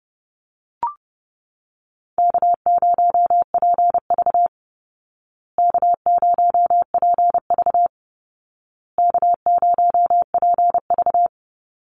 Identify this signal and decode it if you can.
Morse code, heard as a keyed tone, about 30 wpm, 700 Hz. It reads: K0PV K0PV K0PV
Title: Easy Call Signs Repeated 3x - 30wpm